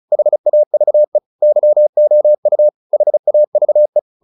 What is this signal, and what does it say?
Morse code, heard as a keyed tone, about 34 wpm, 600 Hz. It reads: HAVE YOU HAVE